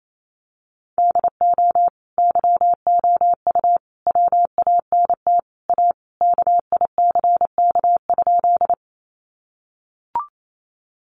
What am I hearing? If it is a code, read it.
Morse code, heard as a keyed tone, about 28 wpm, 700 Hz. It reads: DO YOU WANT A KICK?
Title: Sentences from Top 1300 Words 28wpm